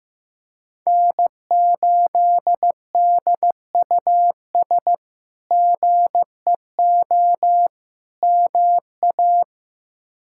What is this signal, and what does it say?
Morse code, heard as a keyed tone, about 15 wpm, 700 Hz. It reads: N8DUS GEO MA
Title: K1USN SST QSO Practice - 15wpm